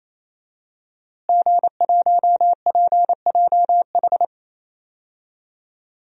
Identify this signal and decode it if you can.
Morse code, heard as a keyed tone, about 28 wpm, 700 Hz. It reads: G1PJH
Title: DXCC-100 Call Signs - 5 Characters - Repeated 3x - 28wpm